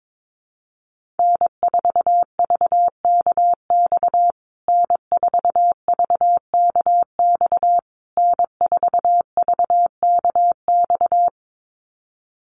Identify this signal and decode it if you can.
Morse code, heard as a keyed tone, about 22 wpm, 700 Hz. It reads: N4VKX N4VKX N4VKX
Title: USA Call Signs Repeated 3x 22wpm